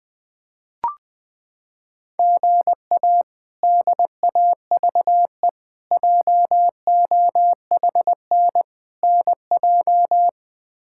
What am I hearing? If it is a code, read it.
Morse code, heard as a keyed tone, about 21 wpm, 700 Hz. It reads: GA DAVE JOHN NJ
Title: K1USN SST QSO Practice - 20wpm